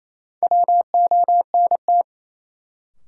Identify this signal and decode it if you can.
Morse code, heard as a keyed tone, about 28 wpm, 700 Hz. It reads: WONT
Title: Sentences from Top 1000 Words - Review Part 6 - New Word Drill - 28wpm